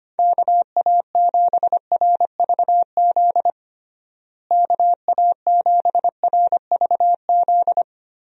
Morse code, heard as a keyed tone, about 25 wpm, 700 Hz. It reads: KA7RVZ KA7RVZ
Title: USA Call Signs - 6 Characters - Repeated 2x - 25wpm